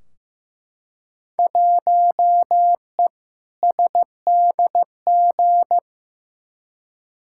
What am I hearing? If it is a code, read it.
Morse code, heard as a keyed tone, about 15 wpm, 700 Hz. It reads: 1E SDG
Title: Field Day Exchanges - 15wpm